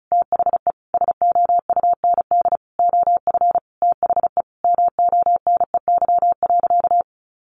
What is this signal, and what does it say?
Morse code, heard as a keyed tone, about 34 wpm, 700 Hz. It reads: THE SOUND OF THE MONEY.
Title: Sentences from Top 1100 Words 35wpm